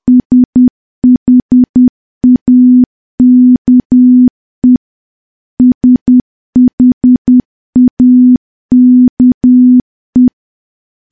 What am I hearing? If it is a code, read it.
Morse code, heard as a keyed tone, about 10 wpm, 250 Hz. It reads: SHAKE SHAKE
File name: morse / morse-training-10WPM